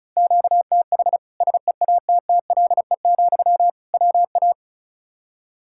Morse code, heard as a keyed tone, about 36 wpm, 700 Hz. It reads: QTH SEATTLE, WA
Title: Practice QSO - Exchange 1 - Beginning - 35wpm